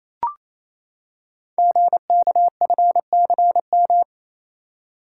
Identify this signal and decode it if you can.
Morse code, heard as a keyed tone, about 27 wpm, 700 Hz. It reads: GKFCM
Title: Code Groups - 5 Letters - 28wpm